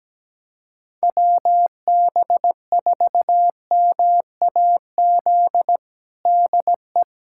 Morse code, heard as a keyed tone, about 17 wpm, 700 Hz. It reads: WB4MAZ DE
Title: USA Callsign DE Callsign - 6 Characters - 17wpm